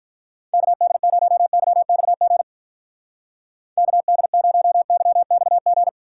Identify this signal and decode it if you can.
Morse code, heard as a keyed tone, about 42 wpm, 700 Hz. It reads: KD0YXG KD0YXG
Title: Speed Racing - USA 2x3 Call Signs 35wpm